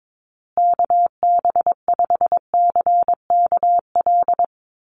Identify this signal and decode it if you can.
Morse code, heard as a keyed tone, about 22 wpm, 700 Hz. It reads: KB5CKL
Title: USA Call Signs Repeated 3x 22wpm